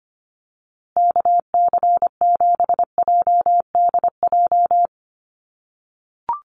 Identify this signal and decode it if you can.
Morse code, heard as a keyed tone, about 25 wpm, 700 Hz. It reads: KC7JDJ
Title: USA Call Signs Repeated 2x 25wpm